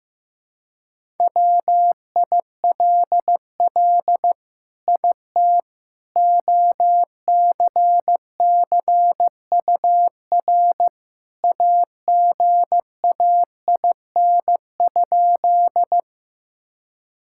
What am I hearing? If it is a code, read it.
Morse code, heard as a keyed tone, about 15 wpm, 700 Hz. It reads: WILL IT OCCUR AGAIN?
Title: Sentences from Top 1000 Words - Review Part 6 - 15wpm